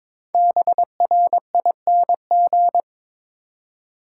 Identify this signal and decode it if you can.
Morse code, heard as a keyed tone, about 22 wpm, 690 Hz. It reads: BRING
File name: 5 Letter Words - First 3 Letters Given - 22wpm